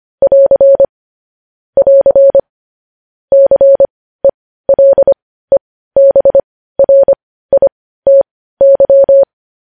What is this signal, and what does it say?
Morse code, heard as a keyed tone, about 25 wpm, spaced as for 15 wpm, 550 Hz. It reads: <AR> <AR> CELEBRITY
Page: news-headlines-in-morse-code-at-15-wpm